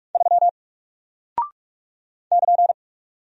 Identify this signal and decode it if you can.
Morse code, heard as a keyed tone, about 44 wpm, 700 Hz. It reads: 3 <KN>
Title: Single Character (Letter:Number:Punctuation:Prosign) Alt 45wpm